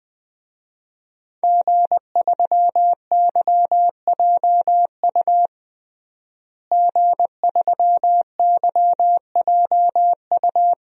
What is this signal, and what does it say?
Morse code, heard as a keyed tone, about 20 wpm, 700 Hz. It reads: G3YJU G3YJU
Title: DXCC-100 Call Signs - 5 Characters - Repeated 3x - 20wpm